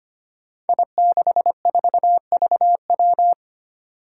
Morse code, heard as a keyed tone, about 25 wpm, 700 Hz. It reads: I64VW
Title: Five Challenging Letter-Number Combinations 25wpm